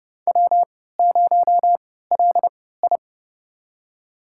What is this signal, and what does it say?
Morse code, heard as a keyed tone, about 30 wpm, spaced as for 17 wpm, 700 Hz. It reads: W0LI
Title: USA Call Signs - 3x Character Spacing - 30wpm